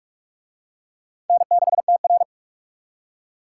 Morse code, heard as a keyed tone, about 41 wpm, 700 Hz. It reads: N6TR